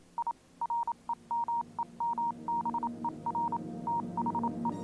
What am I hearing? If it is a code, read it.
Morse code, heard as a keyed tone, about 27 wpm, 940 Hz, with noise. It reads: I REMEMBER THE
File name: Vylet Pony / aura / do you remember the song she sang (morse code version)